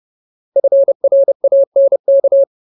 Morse code, heard as a keyed tone, about 30 wpm, 550 Hz. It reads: FRANK
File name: Ultimate Word List / 30WPM Ultimate